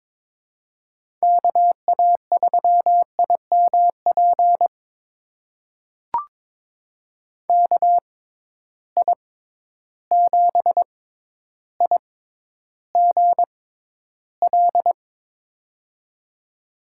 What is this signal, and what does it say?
Morse code, heard as a keyed tone, about 22 wpm, spaced as for 11 wpm, 700 Hz. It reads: KA3IMP KI7IGL
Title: USA Call Signs - 6x Character Spacing - 22wpm